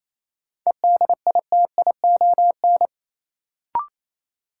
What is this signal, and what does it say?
Morse code, heard as a keyed tone, about 28 wpm, 700 Hz. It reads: EDITION
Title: Sets of 5 Words - New Word Drill - 28wpm